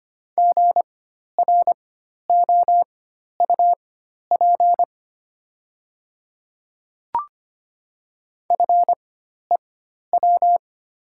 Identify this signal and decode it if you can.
Morse code, heard as a keyed tone, about 25 wpm, spaced as for 12 wpm, 700 Hz. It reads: GROUP FEW
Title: 4x Character Spacing - Top 500 Words - 25wpm